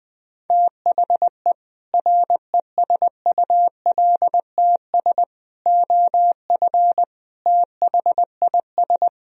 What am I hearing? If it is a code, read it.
Morse code, heard as a keyed tone, about 20 wpm, 700 Hz. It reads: THE RESULTS OF THIS
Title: Speed-Racing - Sets of 5 Words 17wpm